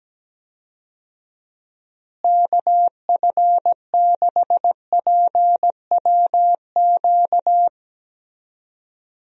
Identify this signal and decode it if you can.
Morse code, heard as a keyed tone, about 17 wpm, 700 Hz. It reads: KF6PWQ
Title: USA Call Signs - 6 Characters - Repeated 3x - 17wpm